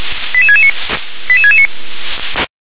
Five-tone selective call (ZVEI), heard as quick runs of five tones, tones of about 70 ms. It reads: 88669 88669